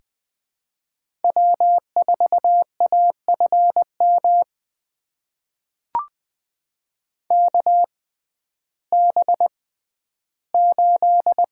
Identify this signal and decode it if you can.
Morse code, heard as a keyed tone, about 20 wpm, spaced as for 12 wpm, 700 Hz. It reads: W4AFM KB8
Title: USA Call Signs - 6x Character Spacing - 20wpm